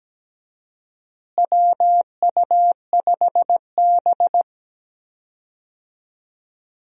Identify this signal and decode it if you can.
Morse code, heard as a keyed tone, about 17 wpm, 700 Hz. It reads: WU5B
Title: USA Call Signs - 4 Characters - Repeated 2x - 17wpm